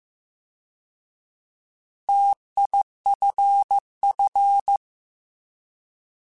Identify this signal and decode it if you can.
Morse code, heard as a keyed tone, about 15 wpm, 800 Hz. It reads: TIFF
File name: Alphabetical / Alphabetical mp3s / Ts @13